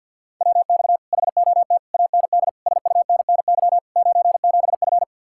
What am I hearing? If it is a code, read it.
Morse code, heard as a keyed tone, about 62 wpm, 700 Hz. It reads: WX HOT AND SUNNY 97F
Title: Speed-Racing Common QSO Phrases Encore - 50wpm